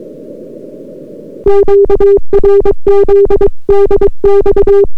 Morse code, heard as a keyed tone, about 22 wpm, 400 Hz. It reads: QRZ DX